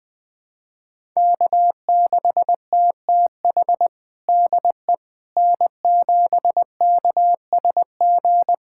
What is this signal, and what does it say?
Morse code, heard as a keyed tone, about 20 wpm, 700 Hz. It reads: K6TTH DE N7KSG
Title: USA Callsign DE Callsign - 5 Characters - 20wpm